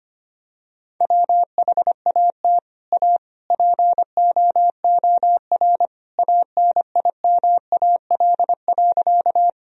Morse code, heard as a keyed tone, about 25 wpm, 700 Hz. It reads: WHAT A POOR ANIMAL.
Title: Sentences from Top 1000 Words - Review Part 1 - 25wpm